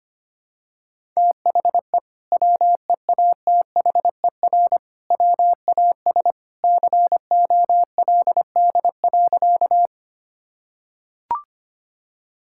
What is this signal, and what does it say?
Morse code, heard as a keyed tone, about 25 wpm, 700 Hz. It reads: THE WEATHER WAS COLD.
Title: Sentences from Top 1000 Words - Review Part 5 - 25wpm